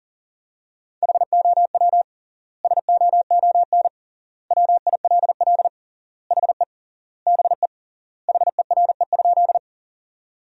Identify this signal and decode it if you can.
Morse code, heard as a keyed tone, about 40 wpm, 700 Hz. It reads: HOW SOON WILL HE BE HERE?
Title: 3x Word-Spacing - Sentences from Top 400 Words - 40wpm